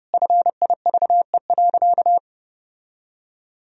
Morse code, heard as a keyed tone, about 29 wpm, 700 Hz. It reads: FIVE.
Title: Sentences from Top 1000 Words - Review Part 2 - 30wpm